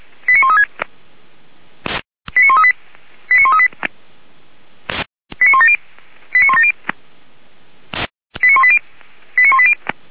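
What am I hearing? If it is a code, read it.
Five-tone selective call (ZVEI), heard as quick runs of five tones, tones of about 70 ms. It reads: 89136 89138 89138 89170 89170 89180 89180
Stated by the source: ZVEI five-tone selective call received by SDR, fire brigade Feuerwehr Orlamünde.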